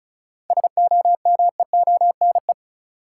Morse code, heard as a keyed tone, about 36 wpm, 700 Hz. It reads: SOMEONE
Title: Sentences from Top 600 Words - New Word Drill - 35wpm